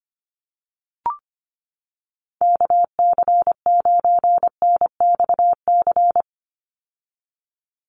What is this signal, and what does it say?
Morse code, heard as a keyed tone, about 25 wpm, 700 Hz. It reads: KC9NXC
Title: USA Call Signs 25wpm